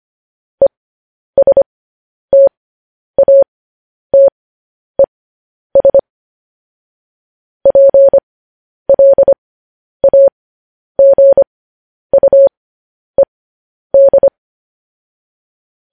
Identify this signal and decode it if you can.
Morse code, heard as a keyed tone, about 25 wpm, spaced as for 10 wpm, 550 Hz. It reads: ESTATES PLAGUED
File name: podcast / News headlines-1x-25x10